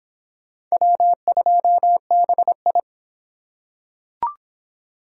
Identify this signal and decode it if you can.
Morse code, heard as a keyed tone, about 26 wpm, 700 Hz. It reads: W2BI